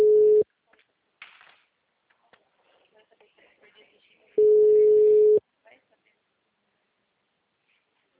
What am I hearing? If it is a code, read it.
Ringback tone (Europe, 425 Hz), with speech.